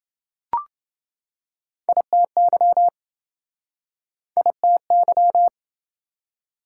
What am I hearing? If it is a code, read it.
Morse code, heard as a keyed tone, about 25 wpm, 700 Hz. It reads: ITY ITY
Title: Speed-Racing Prefixes and Suffixes 20wpm